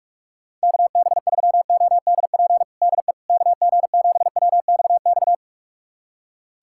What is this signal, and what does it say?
Morse code, heard as a keyed tone, about 42 wpm, 700 Hz. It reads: KB3ODP DE KG7WXX
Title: USA Callsign DE Callsign 45wpm